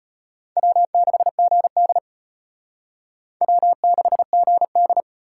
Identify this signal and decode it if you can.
Morse code, heard as a keyed tone, about 32 wpm, 700 Hz. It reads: W6GD W6GD